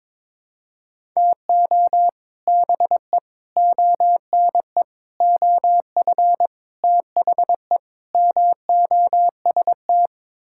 Morse code, heard as a keyed tone, about 22 wpm, 700 Hz. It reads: TO BE ONE OF THE MOST
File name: practice / 22wpm / Sets of 6 Words - 22wpm